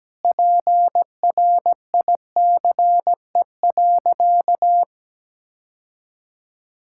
Morse code, heard as a keyed tone, about 17 wpm, 700 Hz. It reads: PRICE.
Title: Sentences from Top 1000 Words - Review Part 6 - 17wpm